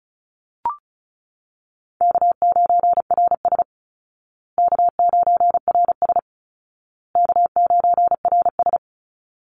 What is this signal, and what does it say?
Morse code, heard as a keyed tone, about 34 wpm, 700 Hz. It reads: K9RS K9RS K9RS
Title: Call Signs Repeated 2x - 35wpm